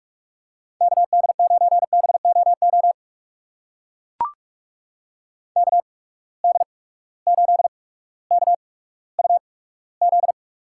Morse code, heard as a keyed tone, about 45 wpm, spaced as for 19 wpm, 700 Hz. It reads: KD9BOO KD8KUZ
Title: USA Call Signs - 6 Characters - 8x Character Spacing - 45wpm